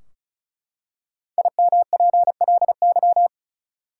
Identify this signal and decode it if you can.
Morse code, heard as a keyed tone, about 36 wpm, 700 Hz. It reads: IMPLY
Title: Five Letter Words - 35wpm